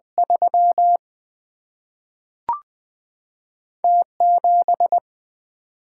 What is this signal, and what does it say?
Morse code, heard as a keyed tone, about 20 wpm, 700 Hz. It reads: 3 T 7
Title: Single-Letter Single-Number 20wpm